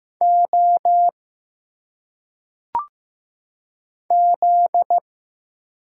Morse code, heard as a keyed tone, about 15 wpm, 700 Hz. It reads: O Z
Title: Single Letters - Phonetic Pronunciation 15wpm